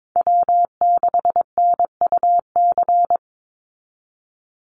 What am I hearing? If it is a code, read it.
Morse code, heard as a keyed tone, about 22 wpm, 700 Hz. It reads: W6NUC
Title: Call Signs Repeated 2x - 22wpm